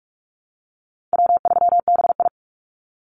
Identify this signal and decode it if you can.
Morse code, heard as a keyed tone, about 44 wpm, 700 Hz. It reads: W3BI
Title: Easy Call Signs Repeated 3x - 45wpm